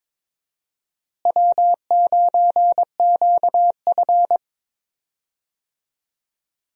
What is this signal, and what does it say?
Morse code, heard as a keyed tone, about 22 wpm, 700 Hz. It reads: W9QF
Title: USA Call Signs - 4 Characters - Repeated 3x - 22wpm